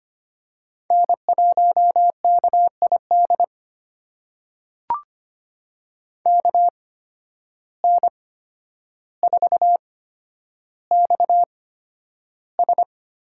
Morse code, heard as a keyed tone, about 25 wpm, spaced as for 11 wpm, 700 Hz. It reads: N1KID KN4XS
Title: USA Call Signs - 5 Characters - 8x Character Spacing - 25wpm